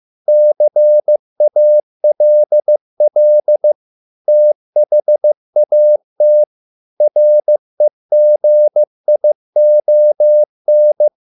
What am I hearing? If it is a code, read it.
Morse code, heard as a keyed tone, about 15 wpm, 600 Hz. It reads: CALL THAT REGION